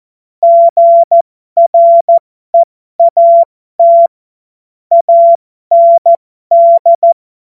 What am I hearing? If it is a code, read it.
Morse code, heard as a keyed tone, about 14 wpm, spaced as for 12 wpm, 680 Hz. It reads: GREAT AND
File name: Morsekurs / Witze